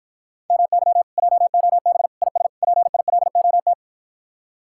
Morse code, heard as a keyed tone, about 52 wpm, 700 Hz. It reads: MY JOB IS PILOT